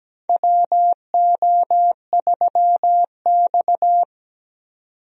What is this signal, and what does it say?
Morse code, heard as a keyed tone, about 17 wpm, 700 Hz. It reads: WO3X
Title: USA Call Signs - 4 Characters - 17wpm